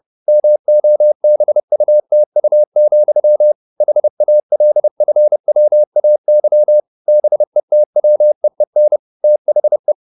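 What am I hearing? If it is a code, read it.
Morse code, heard as a keyed tone, about 30 wpm, 600 Hz. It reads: MOBUTU, HALFWAY BETWEEN THE
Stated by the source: Morse code at 29 wpm, sending Ken Follett's novel Never in English, from CW med Gnister.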